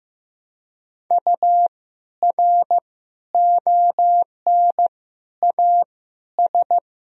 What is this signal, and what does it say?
Morse code, heard as a keyed tone, about 15 wpm, 700 Hz. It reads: U R ON A S
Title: Sentences from Top 400 Words and Abbreviations 15wpm